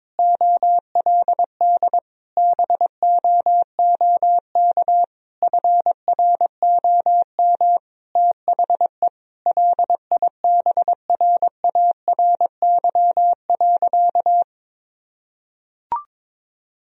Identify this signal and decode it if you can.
Morse code, heard as a keyed tone, about 22 wpm, 700 Hz. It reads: OLD BOOK FROM THE LIBRARY.
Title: Sentences from Top 1000 Words - Review Part 4 - 22wpm